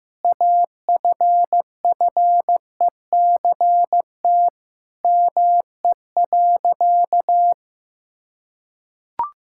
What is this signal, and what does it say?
Morse code, heard as a keyed tone, about 15 wpm, 700 Hz. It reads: AFFECT ME.
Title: Sentences from Top 1000 Words - Review Part 6 - 15wpm